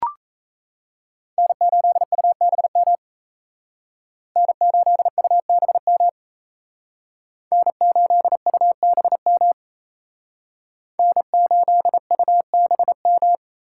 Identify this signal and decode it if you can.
Morse code, heard as a keyed tone, about 34 wpm, 700 Hz. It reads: N8UBM N8UBM N8UBM N8UBM
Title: Speed Racing - USA 1x3 Call Signs 28wpm